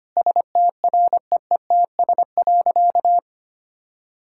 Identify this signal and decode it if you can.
Morse code, heard as a keyed tone, about 25 wpm, 700 Hz. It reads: STREETS.
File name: Easy Sentences with Hints - 25wpm